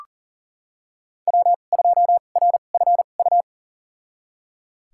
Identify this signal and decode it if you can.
Morse code, heard as a keyed tone, about 40 wpm, spaced as for 29 wpm, 700 Hz. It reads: W2RFU
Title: USA Call Signs - 5 Characters - 2x Character Spacing - 40wpm